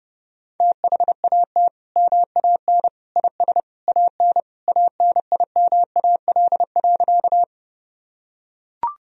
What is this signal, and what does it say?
Morse code, heard as a keyed tone, about 31 wpm, 700 Hz. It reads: THAT MAN IS AN ANIMAL.
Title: Sentences from Top 1000 Words - Review Part 1 - 30wpm